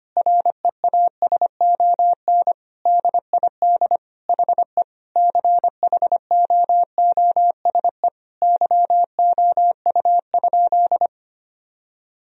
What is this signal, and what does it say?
Morse code, heard as a keyed tone, about 25 wpm, 700 Hz. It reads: REASON DID HE CHOOSE YOU?
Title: Sentences from Top 1000 Words - Review Part 4 - 25wpm